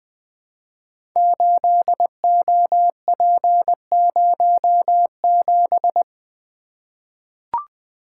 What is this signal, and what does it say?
Morse code, heard as a keyed tone, about 20 wpm, 700 Hz. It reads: 8OP07
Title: Code Groups - 5 Letter:Numbers - 20wpm